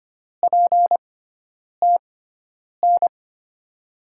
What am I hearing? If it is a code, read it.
Morse code, heard as a keyed tone, about 25 wpm, spaced as for 9 wpm, 700 Hz. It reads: PTN
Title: USA Call Signs - 5 Characters - 6x Character Spacing - 25wpm